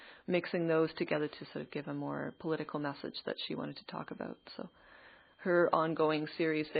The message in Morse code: none

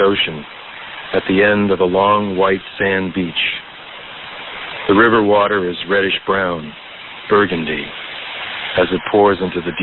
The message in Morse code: none

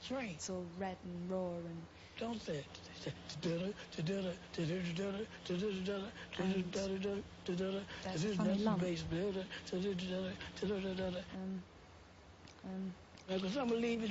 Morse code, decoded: none

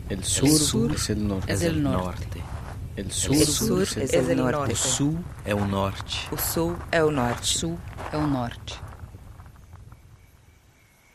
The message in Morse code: none

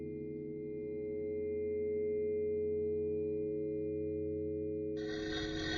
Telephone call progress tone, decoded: dial tone

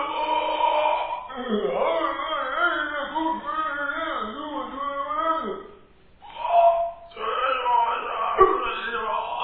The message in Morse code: none

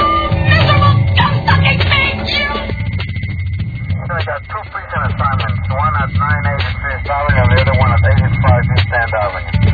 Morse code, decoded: none

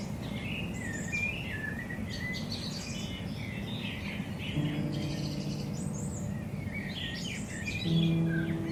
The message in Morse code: none

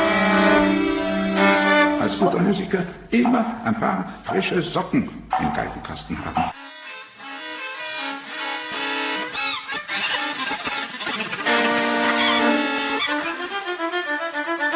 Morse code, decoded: none

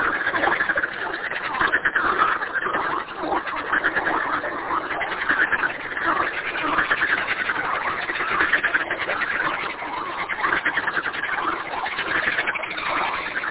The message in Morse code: none

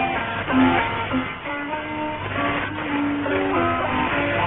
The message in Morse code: none

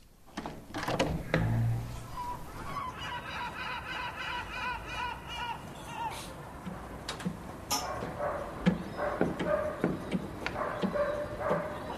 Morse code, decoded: none

